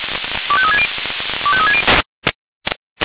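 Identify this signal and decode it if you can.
Five-tone selective call (ZVEI), heard as quick runs of five tones, tones of about 70 ms. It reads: 25360 25360
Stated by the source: ZVEI five-tone selective call received via SDR.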